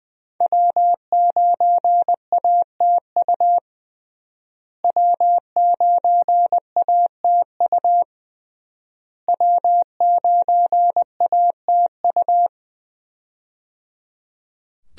Morse code, heard as a keyed tone, about 21 wpm, 700 Hz. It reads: W9ATU W9ATU W9ATU
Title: USA Call Signs - 5 Characters - Repeated 3x - 20wpm